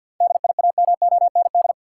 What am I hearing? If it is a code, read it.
Morse code, heard as a keyed tone, about 51 wpm, 700 Hz. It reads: DIAMOND
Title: Seven Letter Words - 50wpm